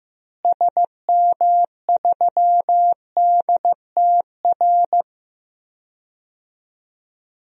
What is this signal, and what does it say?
Morse code, heard as a keyed tone, about 15 wpm, 700 Hz. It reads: SM3DTR
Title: DXCC-100 Call Signs - 6 Characters - Repeated 3x - 15wpm